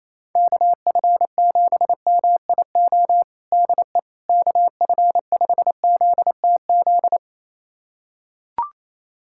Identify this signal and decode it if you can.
Morse code, heard as a keyed tone, about 28 wpm, 700 Hz. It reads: KF7MIO DE KF5ZTZ
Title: USA Callsign DE Callsign 28wpm